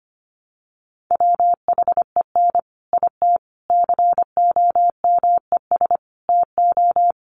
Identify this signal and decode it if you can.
Morse code, heard as a keyed tone, about 25 wpm, 700 Hz. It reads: WHEN IT COMES TO
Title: Sets of 5 Words 25wpm